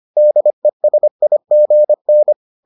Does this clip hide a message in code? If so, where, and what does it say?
Yes: Morse code from 0.2 to 2.3 s, heard as a keyed tone, about 25 wpm, 600 Hz. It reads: DESIGN